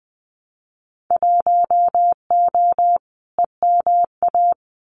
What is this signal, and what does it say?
Morse code, heard as a keyed tone, about 20 wpm, 700 Hz. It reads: 1O EMA
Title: Winter Field Day Exchanges 20wpm